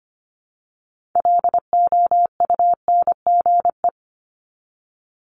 Morse code, heard as a keyed tone, about 25 wpm, 700 Hz. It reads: LOUNGE
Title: Multi-letter Phonemes Repeated 3x 25wpm